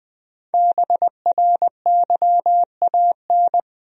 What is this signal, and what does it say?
Morse code, heard as a keyed tone, about 21 wpm, 700 Hz. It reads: BRYAN
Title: USA Common Names - Max 5 Letters - 20wpm